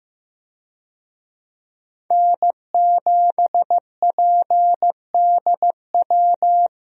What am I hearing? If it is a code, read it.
Morse code, heard as a keyed tone, about 15 wpm, 700 Hz. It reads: N7PDW
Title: USA Call Signs - 5 Characters - 15wpm